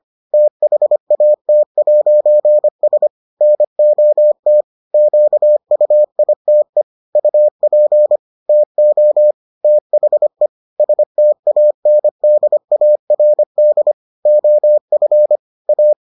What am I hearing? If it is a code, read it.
Morse code, heard as a keyed tone, about 25 wpm, 600 Hz. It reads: THAT'S NOT QUITE UP TO THE STANDARD OF A